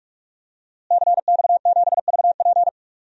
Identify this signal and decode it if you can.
Morse code, heard as a keyed tone, about 45 wpm, 700 Hz. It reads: CX7VP
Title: DXCC-100 Call Signs - 5 Characters - Repeated 2x - 45wpm